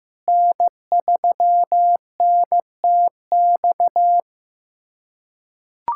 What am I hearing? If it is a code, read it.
Morse code, heard as a keyed tone, about 15 wpm, 700 Hz. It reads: N3NTX
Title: USA Callsign DE Callsign - 5 Characters - 15wpm